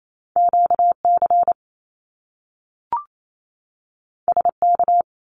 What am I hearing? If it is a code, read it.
Morse code, heard as a keyed tone, about 28 wpm, 700 Hz. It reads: QC SK
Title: Canadian Provinces and Territories 28wpm